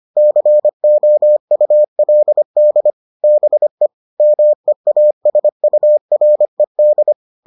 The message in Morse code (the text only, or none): COULD BE MEASURED